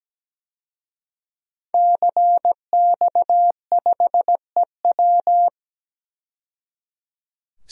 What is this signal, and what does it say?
Morse code, heard as a keyed tone, about 17 wpm, 700 Hz. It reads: CX5EW
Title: DXCC-100 Call Signs - 5 Characters - 17wpm